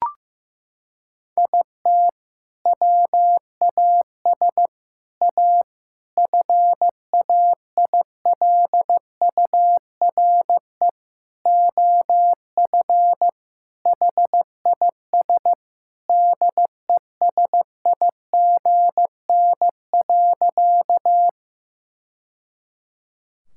Morse code, heard as a keyed tone, about 15 wpm, 700 Hz. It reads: IT WAS A FAILURE OF HIS DESIGN.
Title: Sentences from Top 1000 Words - Review Part 6 - 15wpm